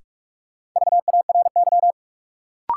Morse code, heard as a keyed tone, about 46 wpm, 700 Hz. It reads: VARY
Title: Sentences from Top 1200 Words - New Word Drill - 45wpm